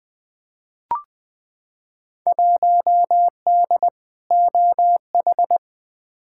Morse code, heard as a keyed tone, about 20 wpm, 700 Hz. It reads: 1D OH
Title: Field Day Exchanges - 20wpm